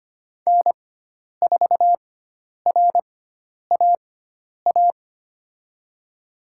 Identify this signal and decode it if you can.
Morse code, heard as a keyed tone, about 25 wpm, spaced as for 10 wpm, 700 Hz. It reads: N4RAA
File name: Lesson 16g - Letter c - Call Signs 25wpm